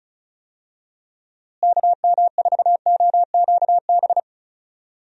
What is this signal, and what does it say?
Morse code, heard as a keyed tone, about 36 wpm, 700 Hz. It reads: KM4OQB
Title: USA Call Signs - 6 Characters - Repeated 2x - 35wpm